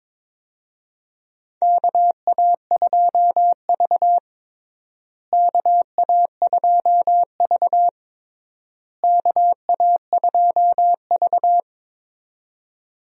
Morse code, heard as a keyed tone, about 22 wpm, 700 Hz. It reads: KA2V KA2V KA2V
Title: USA Call Signs - 4 Characters - Repeated 3x - 22wpm